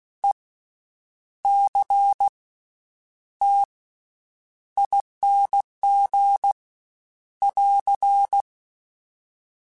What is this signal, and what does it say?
Morse code, heard as a keyed tone, about 16 wpm, spaced as for 8 wpm, 800 Hz. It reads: ECTING <AR>
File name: Special INGs / Special INGs-mp3s / VerySlowToFaster Ing @15